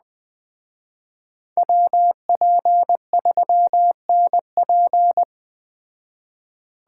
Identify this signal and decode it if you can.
Morse code, heard as a keyed tone, about 20 wpm, 700 Hz. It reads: WP3NP